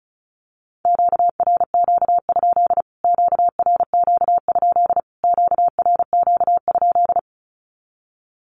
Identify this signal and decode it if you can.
Morse code, heard as a keyed tone, about 34 wpm, 700 Hz. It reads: QRQ? QRQ? QRQ?
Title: Q-Codes Repeated 3x 35wpm